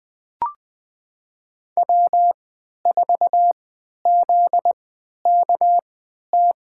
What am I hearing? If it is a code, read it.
Morse code, heard as a keyed tone, about 21 wpm, spaced as for 11 wpm, 700 Hz. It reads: W4ZKT
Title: USA Call Signs - 3x Character Spacing - 20wpm